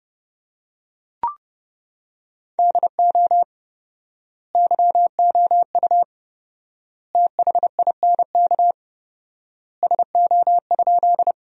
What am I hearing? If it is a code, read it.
Morse code, heard as a keyed tone, about 30 wpm, 700 Hz. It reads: DO YOU THINK SO?
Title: 4x Word-Spacing - Sentences from Top 200 Words - 30wpm